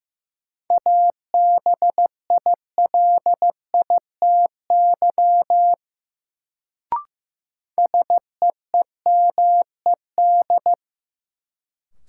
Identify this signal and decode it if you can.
Morse code, heard as a keyed tone, about 15 wpm, 700 Hz. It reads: ABILITY SEEMED
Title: Sets of 3 Words - New Word Drill - 15wpm